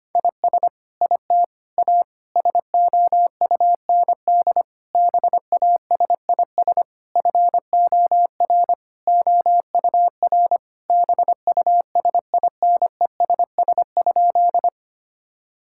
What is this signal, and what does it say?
Morse code, heard as a keyed tone, about 25 wpm, 700 Hz. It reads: IS IT A SOUND BASIS FOR OUR BUSINESS?
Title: Sentences from Top 1000 Words - Review Part 5 - 25wpm